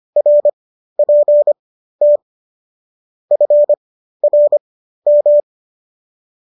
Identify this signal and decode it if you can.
Morse code, heard as a keyed tone, about 25 wpm, spaced as for 13 wpm, 590 Hz. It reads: RPT FRM